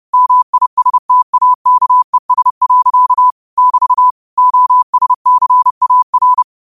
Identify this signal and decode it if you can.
Morse code, heard as a keyed tone, about 30 wpm, 1000 Hz. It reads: MISTAKES. <BT> OSCAR